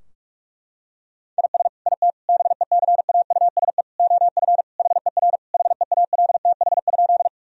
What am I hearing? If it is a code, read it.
Morse code, heard as a keyed tone, about 41 wpm, 700 Hz. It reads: IS IT BECAUSE OF HER HEALTH?
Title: Sentences from Top 1000 Words - Review Part 3 - 45wpm